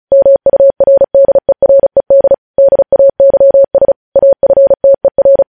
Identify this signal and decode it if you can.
Morse code, heard as a keyed tone, about 36 wpm, 550 Hz. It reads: MURDERED DAYS AFTER